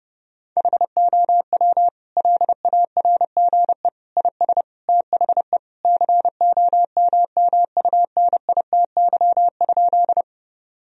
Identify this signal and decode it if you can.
Morse code, heard as a keyed tone, about 31 wpm, 700 Hz. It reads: HOW LARGE IS THE COMMUNITY?
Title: Sentences from Top 1000 Words - Review Part 4 - 30wpm